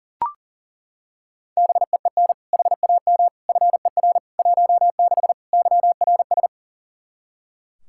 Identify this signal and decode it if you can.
Morse code, heard as a keyed tone, about 40 wpm, 700 Hz. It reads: BEEN HAM FER 16 YRS
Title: K1USN SST QSO Practice 40wpm